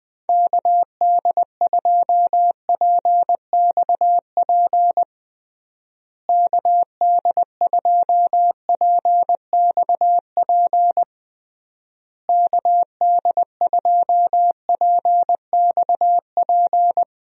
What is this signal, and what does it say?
Morse code, heard as a keyed tone, about 20 wpm, 700 Hz. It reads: KD2PXP KD2PXP KD2PXP
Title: USA Call Signs - 6 Characters - Repeated 3x - 20wpm